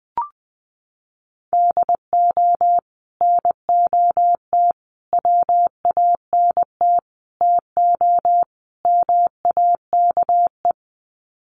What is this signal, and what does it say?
Morse code, heard as a keyed tone, about 20 wpm, 700 Hz. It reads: DO NOT WANT TO MAKE
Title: Sets of 5 Words Encore 20wpm